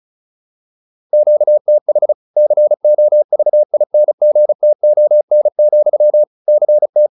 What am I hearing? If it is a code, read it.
Morse code, heard as a keyed tone, about 35 wpm, 600 Hz. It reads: QTH COVINGTON, CT
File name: QSO403 35